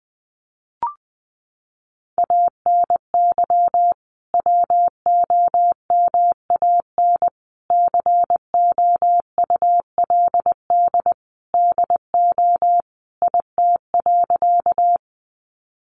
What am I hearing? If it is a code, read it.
Morse code, heard as a keyed tone, about 20 wpm, 700 Hz. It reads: ANY WOMAN COULD DO IT.
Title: Sentences from Top 1100 Words 20wpm